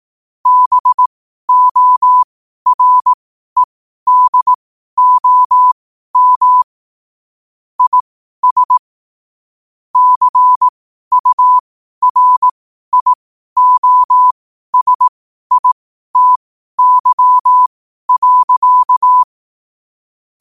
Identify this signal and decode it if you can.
Morse code, heard as a keyed tone, about 18 wpm, spaced as for 12 wpm, 1000 Hz. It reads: BOREDOM IS CURIOSITY.
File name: qotdfiles / 12wpm